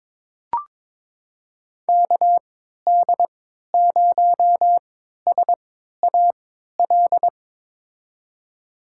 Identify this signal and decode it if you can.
Morse code, heard as a keyed tone, about 22 wpm, spaced as for 12 wpm, 700 Hz. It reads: KD0SAL
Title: USA Call Signs - 6 Characters - 3x Character Spacing - 22wpm